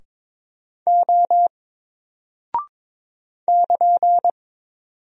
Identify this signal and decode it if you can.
Morse code, heard as a keyed tone, about 22 wpm, 700 Hz. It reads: O <KN>
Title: Single Character (Letter:Number:Punctuation:Prosign) Alt 22wpm